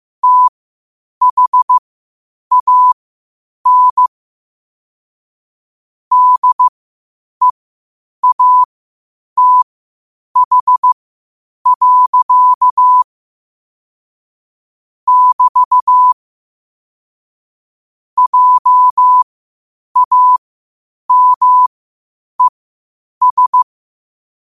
Morse code, heard as a keyed tone, about 15 wpm, spaced as for 8 wpm, 1000 Hz. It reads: THAN DEATH. <BT> JAMES
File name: qotdfiles / 8wpm